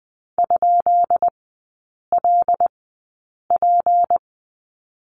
Morse code, heard as a keyed tone, about 20 wpm, spaced as for 8 wpm, 700 Hz. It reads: ?LP
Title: 191 Lesson 31e - Letter v - Three Characters 20wpm